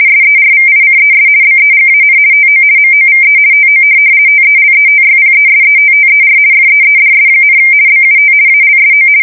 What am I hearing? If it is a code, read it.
RTTY, about 50 baud. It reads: SERVICES BEGAN TO MOVE TO TELETYPESETTER FEEDS, BUT ALSO
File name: fsk170-50B / teletype-corp